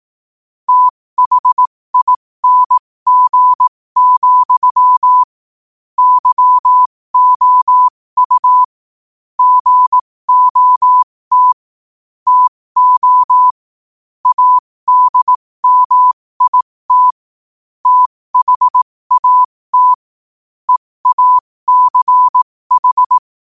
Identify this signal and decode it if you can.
Morse code, heard as a keyed tone, about 18 wpm, spaced as for 15 wpm, 1000 Hz. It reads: THING, YOU GOT TO ADMIT THAT EACH
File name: qotdfiles / 15wpm